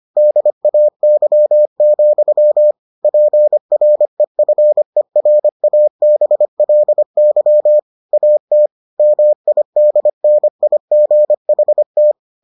Morse code, heard as a keyed tone, about 25 wpm, 600 Hz. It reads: DAY, PREFERABLY AT MIDNIGHT